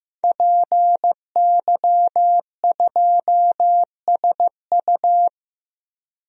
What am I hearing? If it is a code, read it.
Morse code, heard as a keyed tone, about 15 wpm, 700 Hz. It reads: PY2SU
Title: DXCC-100 Call Signs - 5 Characters - Repeated 2x - 15wpm